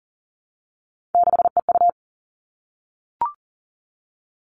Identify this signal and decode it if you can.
Morse code, heard as a keyed tone, about 40 wpm, 700 Hz. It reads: 6EU
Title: All Cut Numbers - 3-Digit 40wpm